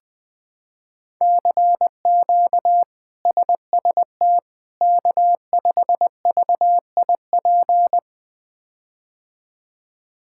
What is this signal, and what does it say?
Morse code, heard as a keyed tone, about 20 wpm, 700 Hz. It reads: CQ SST K5VIP
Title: K1USN SST QSO Practice - 20wpm